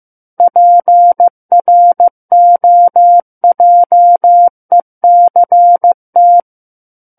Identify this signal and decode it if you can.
Morse code, heard as a keyed tone, about 15 wpm, 700 Hz. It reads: PROJECT
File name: Book One / 06-Chapter 6 - 15 WPM